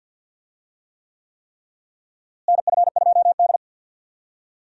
Morse code, heard as a keyed tone, about 51 wpm, 700 Hz. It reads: NF2D
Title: USA Call Signs - 4 Characters - Repeated 3x - 50wpm